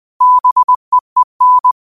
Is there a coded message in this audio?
Yes: Morse code from 0.2 to 1.7 s, heard as a keyed tone, about 20 wpm, 1000 Hz. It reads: BEEN